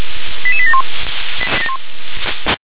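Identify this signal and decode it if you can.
Five-tone selective call (ZVEI), heard as quick runs of five tones, tones of about 70 ms. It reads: 88871
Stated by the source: ZVEI five-tone selective call received on SDR.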